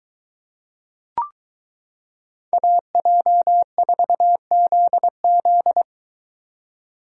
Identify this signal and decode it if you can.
Morse code, heard as a keyed tone, about 22 wpm, 700 Hz. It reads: AJ4ZZ